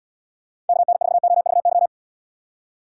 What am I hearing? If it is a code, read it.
Morse code, heard as a keyed tone, about 73 wpm, 700 Hz. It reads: KN4OVY